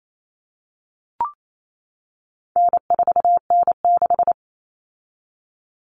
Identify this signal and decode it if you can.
Morse code, heard as a keyed tone, about 28 wpm, 700 Hz. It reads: N4N6
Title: Common Cut Numbers - 4-Digit 28wpm